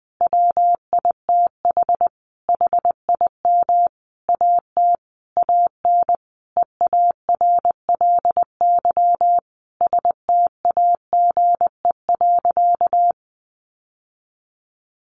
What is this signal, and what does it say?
Morse code, heard as a keyed tone, about 20 wpm, 700 Hz. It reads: WITH HIM AT AN EARLY STAGE.
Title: Sentences from Top 1200 Words 20wpm